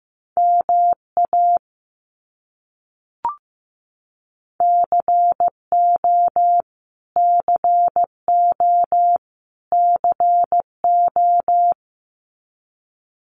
Morse code, heard as a keyed tone, about 15 wpm, 700 Hz. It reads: MA CO CO CO
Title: US States Repeated 3x 15wpm